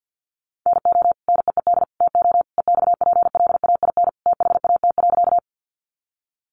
Morse code, heard as a keyed tone, about 50 wpm, 700 Hz. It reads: NO NEED TO EXPLAIN THAT.
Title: Sentences from Top 1100 Words 50wpm